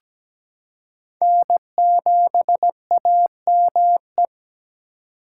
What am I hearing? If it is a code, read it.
Morse code, heard as a keyed tone, about 17 wpm, 700 Hz. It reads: N7AME
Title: K1USN SST QSO Practice - 17wpm